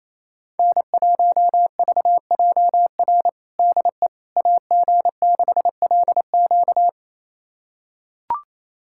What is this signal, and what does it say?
Morse code, heard as a keyed tone, about 28 wpm, 700 Hz. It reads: N1VJR DE AG6LQ
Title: USA Callsign DE Callsign - 5 Characters - 28wpm